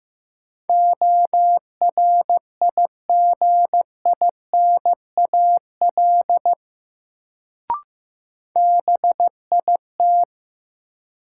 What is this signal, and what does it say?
Morse code, heard as a keyed tone, about 16 wpm, 700 Hz. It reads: ORIGINAL BIT
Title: Sentences from Top 900 Words - New Word Drill - 15wpm